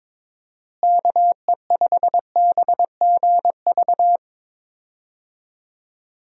Morse code, heard as a keyed tone, about 22 wpm, 700 Hz. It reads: KE5BGV
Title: USA Call Signs - 6 Characters - Repeated 3x - 22wpm